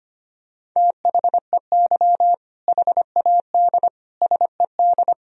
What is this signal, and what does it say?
Morse code, heard as a keyed tone, about 25 wpm, 700 Hz. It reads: THEY HAD SED
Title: Sentences from Top 400 Words and Abbreviations 25wpm